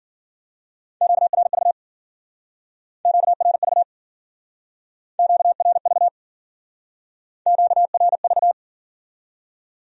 Morse code, heard as a keyed tone, about 53 wpm, 700 Hz. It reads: QRV QRV QRV QRV